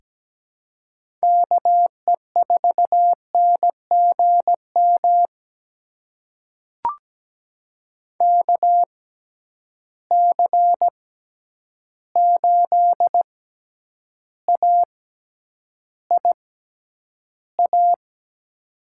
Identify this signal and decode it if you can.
Morse code, heard as a keyed tone, about 17 wpm, spaced as for 9 wpm, 700 Hz. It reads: KE4NGM KC8AIA
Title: USA Call Signs - 6 Characters - 6x Character Spacing - 17wpm